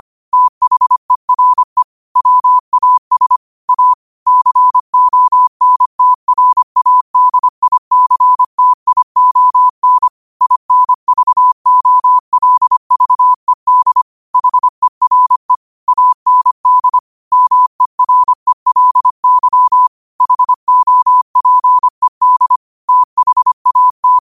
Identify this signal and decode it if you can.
Morse code, heard as a keyed tone, about 25 wpm, 1000 Hz. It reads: THERE WAS A CONTRADICTION INVOLVED HERE AND MERELY HOPED THAT